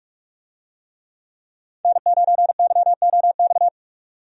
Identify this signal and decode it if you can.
Morse code, heard as a keyed tone, about 46 wpm, 700 Hz. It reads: N9YOX
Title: USA Call Signs - 5 Characters - 45wpm